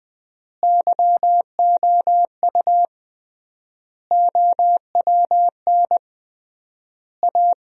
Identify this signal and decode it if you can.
Morse code, heard as a keyed tone, about 20 wpm, 700 Hz. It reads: YOU OWN A
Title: 3x Word-Spacing - Sentences from Top 400 Words - 20wpm